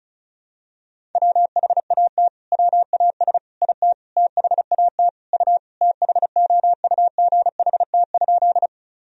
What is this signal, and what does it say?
Morse code, heard as a keyed tone, about 36 wpm, 700 Hz. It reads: WHAT WAS IT THAT U THOUGHT?
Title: Sentences from Top 300 Words and Abbreviations 35wpm